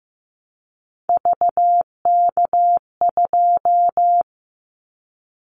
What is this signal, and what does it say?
Morse code, heard as a keyed tone, about 15 wpm, 700 Hz. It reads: VK2
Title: Callsign Prefixes for Top 100 DXCC - 15wpm